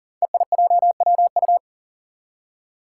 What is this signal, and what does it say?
Morse code, heard as a keyed tone, about 40 wpm, 700 Hz. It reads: EIJWU
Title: Five Challenging Letter-Number Combinations 40wpm